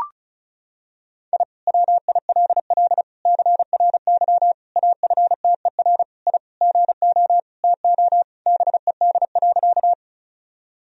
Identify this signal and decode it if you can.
Morse code, heard as a keyed tone, about 34 wpm, 700 Hz. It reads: I WILL CRY AFTER I GO TO BED.
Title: Sentences from Top 1000 Words - Review Part 2 - 35wpm